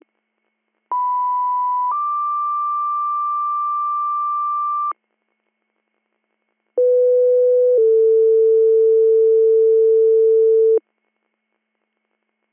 Two-tone page, heard as a one-second tone, then a three-second tone, about 990 then 1180 Hz, about 500 then 450 Hz. It is heard from 0.9 to 10.8 s.